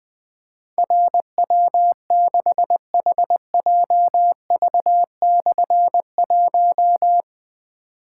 Morse code, heard as a keyed tone, about 21 wpm, 700 Hz. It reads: RW6HJV/1
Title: Difficult Call Signs 20wpm